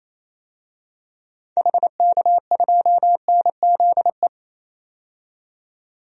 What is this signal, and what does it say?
Morse code, heard as a keyed tone, about 28 wpm, 700 Hz. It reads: HK2NZE
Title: DXCC-100 Call Signs - 6 Characters - 28wpm